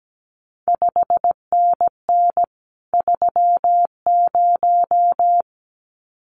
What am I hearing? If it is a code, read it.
Morse code, heard as a keyed tone, about 17 wpm, 700 Hz. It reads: 5NN 30
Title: CQ WW DX Contest Exchange Encore - 17wpm